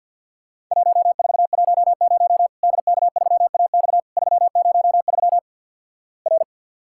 Morse code, heard as a keyed tone, about 50 wpm, 700 Hz. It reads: 1410 DK3AX 303 R
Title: Worked All Europe DX Contest - QTC Exchange - 50wpm